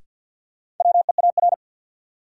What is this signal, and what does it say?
Morse code, heard as a keyed tone, about 51 wpm, 700 Hz. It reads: WEAR
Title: Sentences from Top 1000 Words - Review Part 6 - New Word Drill - 50wpm